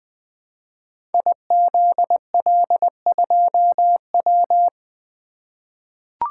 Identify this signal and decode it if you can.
Morse code, heard as a keyed tone, about 20 wpm, 700 Hz. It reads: IZL2W
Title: Code Groups - 5 Letter:Numbers - 20wpm